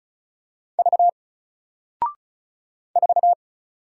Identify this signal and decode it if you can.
Morse code, heard as a keyed tone, about 36 wpm, 700 Hz. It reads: V 4
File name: Single-Letter-Number-35wpm